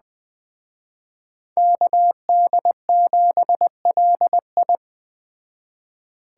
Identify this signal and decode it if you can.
Morse code, heard as a keyed tone, about 20 wpm, 700 Hz. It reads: KD7LI